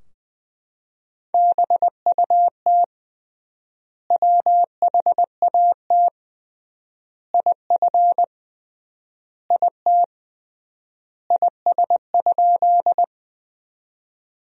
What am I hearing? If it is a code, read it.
Morse code, heard as a keyed tone, about 21 wpm, 700 Hz. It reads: BUT WHAT IF IT IS?
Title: 3x Word-Spacing - Sentences from Top 100 Words - 20wpm